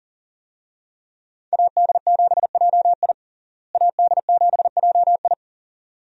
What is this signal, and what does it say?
Morse code, heard as a keyed tone, about 37 wpm, 700 Hz. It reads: AD7JI AD7JI
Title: USA Call Signs - 5 Characters - Repeated 2x - 40wpm